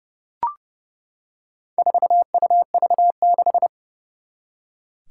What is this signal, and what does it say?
Morse code, heard as a keyed tone, about 29 wpm, 690 Hz. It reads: 4UV6
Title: Four Challenging Letter-Number Combinations 30wpm